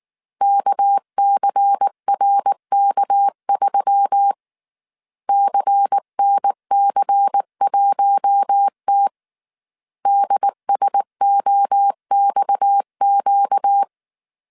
Morse code, heard as a keyed tone, about 19 wpm, 800 Hz. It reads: KCRK3 CNC1T DSOXQ
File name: CW / DET / 1 6 1500C S A 18wpm 2255